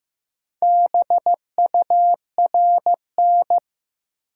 Morse code, heard as a keyed tone, about 15 wpm, 700 Hz. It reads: BURN
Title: Sentences from Top 1300 Words - New Word Drill - 15wpm